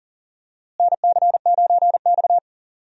Morse code, heard as a keyed tone, about 40 wpm, 700 Hz. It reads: NC9X
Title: USA Call Signs - 4 Characters - Repeated 3x - 40wpm